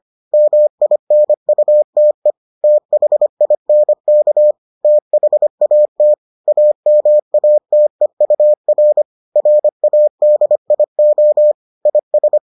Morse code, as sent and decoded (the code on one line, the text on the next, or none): -- .. -. ..- - . / - .... .. -. -.- / - .... .- - / .- -- .- - . ..- .-. / .-. .- -.. .. --- / .. ...
MINUTE THINK THAT AMATEUR RADIO IS